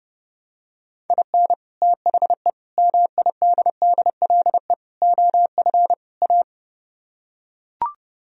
Morse code, heard as a keyed tone, about 30 wpm, 700 Hz. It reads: IN THE MIDDLE OF A